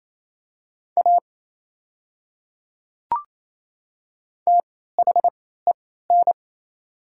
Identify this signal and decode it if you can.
Morse code, heard as a keyed tone, about 27 wpm, spaced as for 16 wpm, 690 Hz. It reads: A THEN
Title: 3x Character Spacing - Top 100 Words - 28wpm